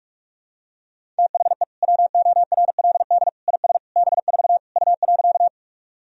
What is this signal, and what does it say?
Morse code, heard as a keyed tone, about 42 wpm, 700 Hz. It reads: THE WORLD IS B4 U.
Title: Sentences from Top 200 Words and Abbreviations 45wpm